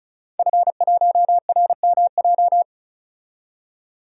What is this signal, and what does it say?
Morse code, heard as a keyed tone, about 34 wpm, 700 Hz. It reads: F1RMJ
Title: DXCC-100 Call Signs - 5 Characters - 35wpm